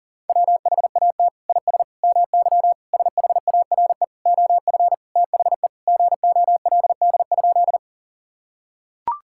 Morse code, heard as a keyed tone, about 40 wpm, 700 Hz. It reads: WHAT IS MY SHARE OF THE GOLD?
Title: Sentences from Top 1000 Words - Review Part 6 - 40wpm